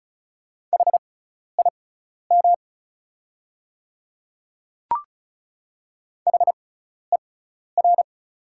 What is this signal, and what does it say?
Morse code, heard as a keyed tone, about 36 wpm, spaced as for 12 wpm, 690 Hz. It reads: HIM HER
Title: 6x Character Spacing - Top 200 Words - 35wpm